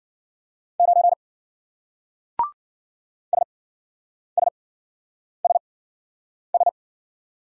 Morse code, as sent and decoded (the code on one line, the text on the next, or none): ----. / ... ... ... ...
9 SSSS